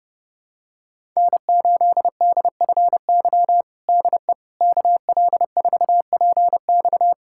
Morse code, heard as a keyed tone, about 30 wpm, 700 Hz. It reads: N8DFY DE KL4PX
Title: USA Callsign DE Callsign - 5 Characters - 30wpm